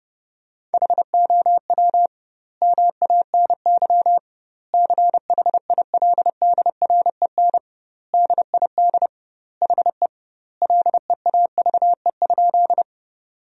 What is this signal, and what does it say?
Morse code, heard as a keyed tone, about 30 wpm, 700 Hz. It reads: HOW MANY CHILDREN DID HE LEAVE?
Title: 2x Word-Spacing - Sentences from Top 400 Words - 30wpm